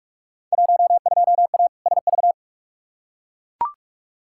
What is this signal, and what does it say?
Morse code, heard as a keyed tone, about 41 wpm, 700 Hz. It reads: 12A SV
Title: Field Day Exchanges Encore - 45wpm